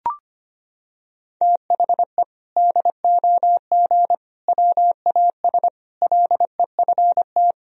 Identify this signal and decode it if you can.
Morse code, heard as a keyed tone, about 25 wpm, 700 Hz. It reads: THE DOG WAS LEFT
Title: Sentences from Top 1000 Words - Review Part 2 - 25wpm